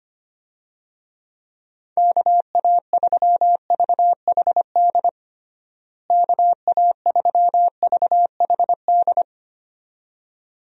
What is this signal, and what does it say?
Morse code, heard as a keyed tone, about 25 wpm, 700 Hz. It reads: KA3VHD KA3VHD
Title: USA Call Signs - 6 Characters - Repeated 2x - 25wpm